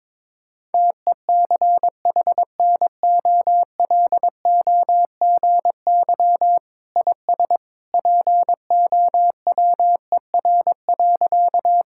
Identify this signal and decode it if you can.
Morse code, heard as a keyed tone, about 22 wpm, 700 Hz. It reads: TECHNOLOGY IS POWER.
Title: Sentences from Top 1000 Words - Review Part 5 - 22wpm